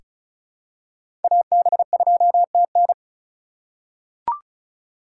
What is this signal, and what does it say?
Morse code, heard as a keyed tone, about 34 wpm, 700 Hz. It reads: AB2TN
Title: USA Call Signs - 5 Characters - 4x Character Spacing - 35wpm